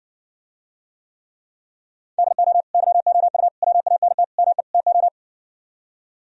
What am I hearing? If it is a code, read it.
Morse code, heard as a keyed tone, about 60 wpm, 700 Hz. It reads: DO YOU WANT ME TO